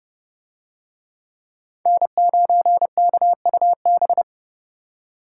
Morse code, heard as a keyed tone, about 28 wpm, 700 Hz. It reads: N9KUB